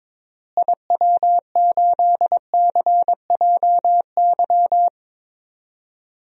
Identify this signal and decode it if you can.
Morse code, heard as a keyed tone, about 22 wpm, 700 Hz. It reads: IW8CJY
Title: DXCC-100 Call Signs - 6 Characters - Repeated 2x - 22wpm